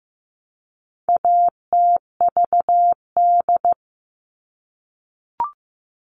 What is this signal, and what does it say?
Morse code, heard as a keyed tone, about 15 wpm, 700 Hz. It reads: ATVD
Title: All Cut Numbers - 4-Digit 15wpm